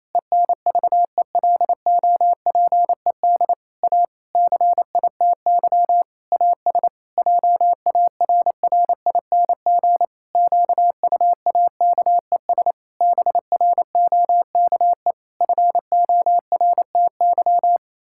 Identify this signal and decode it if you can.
Morse code, heard as a keyed tone, about 28 wpm, 700 Hz. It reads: ENVELOPED A CITY AS JARRING QUAKES BROKE FORTY
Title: Pangram Challenge v2 - 28wpm